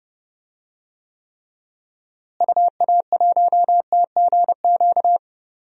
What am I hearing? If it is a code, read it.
Morse code, heard as a keyed tone, about 31 wpm, 700 Hz. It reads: UA1TGQ
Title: DXCC-100 Call Signs - 6 Characters - Repeated 2x - 30wpm